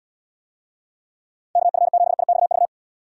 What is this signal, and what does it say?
Morse code, heard as a keyed tone, about 79 wpm, 700 Hz. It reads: KF7IXV